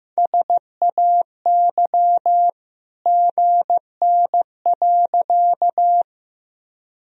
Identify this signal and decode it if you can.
Morse code, heard as a keyed tone, about 14 wpm, 700 Hz. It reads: SAY GN.
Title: Sentences from Top 300 Words and Abbreviations 15wpm